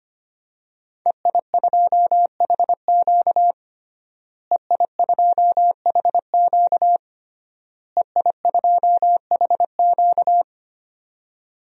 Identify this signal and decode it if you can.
Morse code, heard as a keyed tone, about 25 wpm, 700 Hz. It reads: EI2HQ EI2HQ EI2HQ
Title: DXCC-100 Call Signs - 5 Characters - Repeated 3x - 25wpm